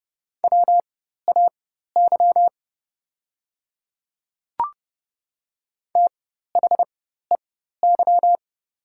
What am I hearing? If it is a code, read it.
Morse code, heard as a keyed tone, about 30 wpm, spaced as for 14 wpm, 690 Hz. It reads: WAY THEY
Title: 4x Character Spacing - Top 300 Words - 30wpm